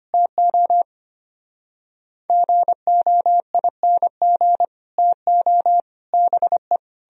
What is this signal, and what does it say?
Morse code, heard as a keyed tone, about 33 wpm, 700 Hz. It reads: TO GOING TO BE